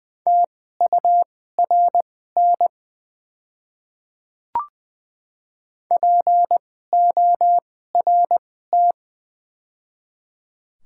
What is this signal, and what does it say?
Morse code, heard as a keyed tone, about 21 wpm, spaced as for 15 wpm, 700 Hz. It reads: TURN PORT
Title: 2x Character Spacing - Top 200 Words - 20wpm